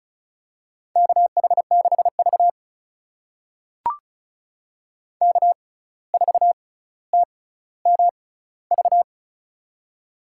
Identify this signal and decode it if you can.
Morse code, heard as a keyed tone, about 36 wpm, spaced as for 17 wpm, 700 Hz. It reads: KH6V K4TMV
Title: USA Call Signs - 6x Character Spacing - 35wpm